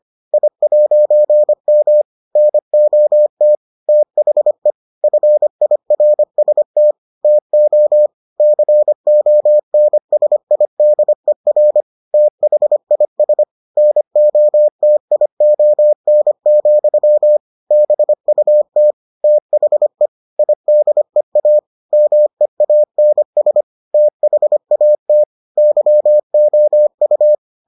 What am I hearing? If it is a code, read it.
Morse code, heard as a keyed tone, about 25 wpm, 600 Hz. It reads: I'M NOT THE FIRST TO CONSIDER THIS NOTION, BUT THE IDEA MEANS THAT YOU